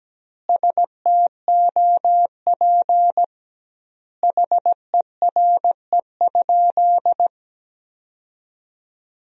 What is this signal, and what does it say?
Morse code, heard as a keyed tone, about 17 wpm, 700 Hz. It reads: STOP HERE?
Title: 2x Word-Spacing - Sentences from Top 300 Words - 17wpm